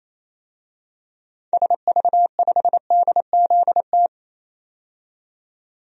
Morse code, heard as a keyed tone, about 28 wpm, 700 Hz. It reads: SV5DZT
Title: DXCC-100 Call Signs - 6 Characters - 28wpm